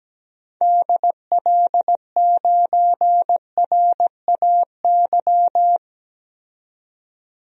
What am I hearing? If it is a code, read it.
Morse code, heard as a keyed tone, about 17 wpm, 700 Hz. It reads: DL9RAY
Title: DXCC-100 Call Signs - 6 Characters - 17wpm